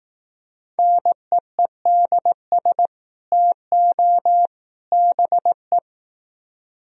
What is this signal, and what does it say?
Morse code, heard as a keyed tone, about 18 wpm, 700 Hz. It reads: NEEDS TO BE
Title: Speed-Racing - Sets of 3 Words 15wpm